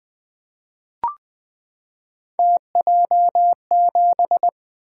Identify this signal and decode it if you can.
Morse code, heard as a keyed tone, about 20 wpm, 700 Hz. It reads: TJ7
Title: Three Letter-Number Combinations - 20wpm